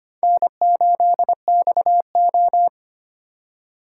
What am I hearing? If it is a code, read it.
Morse code, heard as a keyed tone, about 32 wpm, 700 Hz. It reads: N8XO